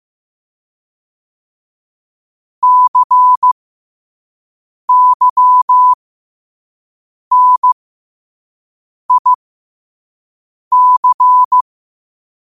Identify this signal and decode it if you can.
Morse code, heard as a keyed tone, about 15 wpm, spaced as for 5 wpm, 1000 Hz. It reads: CYNIC
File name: qotdfiles / 5wpm